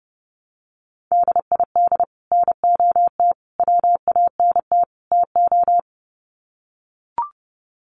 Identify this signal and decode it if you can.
Morse code, heard as a keyed tone, about 30 wpm, 700 Hz. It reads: DID NOT WANT TO
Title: Sets of 4 Words 30wpm